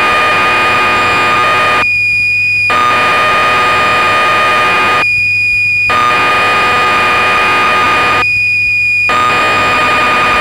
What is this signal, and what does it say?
Cassette data (Kansas City type).